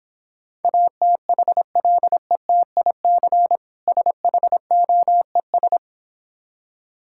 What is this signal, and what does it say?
Morse code, heard as a keyed tone, about 26 wpm, 700 Hz. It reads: ATHLETIC SHOES